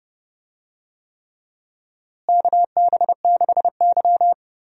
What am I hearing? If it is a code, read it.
Morse code, heard as a keyed tone, about 32 wpm, 700 Hz. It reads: KB6Y
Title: USA Call Signs - 4 Characters - Repeated 3x - 30wpm